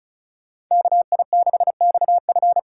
Morse code, heard as a keyed tone, about 43 wpm, 700 Hz. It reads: KI6XF
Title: Speed Racing - USA 2x2 Call Signs 35wpm